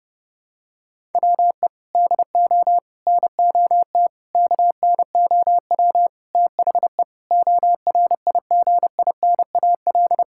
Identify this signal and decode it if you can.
Morse code, heard as a keyed tone, about 30 wpm, 700 Hz. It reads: WE DO NOT KNOW THE ORIGINAL
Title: Sentences from Top 1000 Words - Review Part 6 - 30wpm